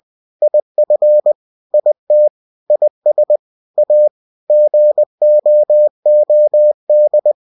Morse code, heard as a keyed tone, about 20 wpm, 600 Hz. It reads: IF IT IS A GOOD
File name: cw-books / winnie-the-pooh / winnie-20wpm-page-0045